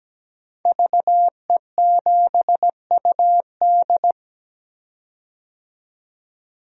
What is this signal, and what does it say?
Morse code, heard as a keyed tone, about 17 wpm, 700 Hz. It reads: VE7UD
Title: DXCC-100 Call Signs - 5 Characters - 17wpm